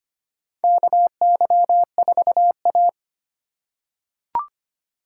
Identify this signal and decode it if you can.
Morse code, heard as a keyed tone, about 25 wpm, 700 Hz. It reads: KY4A
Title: USA Call Signs - 4 Characters - 3x Character Spacing - 25wpm